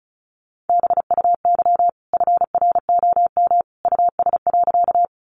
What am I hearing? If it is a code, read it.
Morse code, heard as a keyed tone, about 34 wpm, 700 Hz. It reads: BUY FROM US.
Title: Sentences from Top 700 Words 35wpm